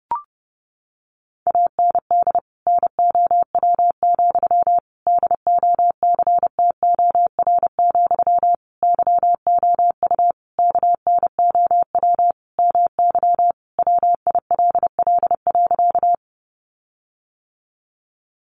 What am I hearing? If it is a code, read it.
Morse code, heard as a keyed tone, about 30 wpm, 700 Hz. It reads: AND NOW, DOCTOR, YOU KNOW MY WILL.
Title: Sentences from Top 1500 Words 30wpm